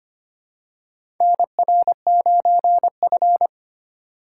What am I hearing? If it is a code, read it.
Morse code, heard as a keyed tone, about 25 wpm, 700 Hz. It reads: NR9F
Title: USA Call Signs - 2x Character Spacing - 25wpm